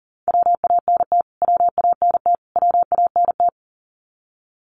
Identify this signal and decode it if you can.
Morse code, heard as a keyed tone, about 40 wpm, 700 Hz. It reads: WANT WANT WANT
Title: Top 500 Words Repeated 40wpm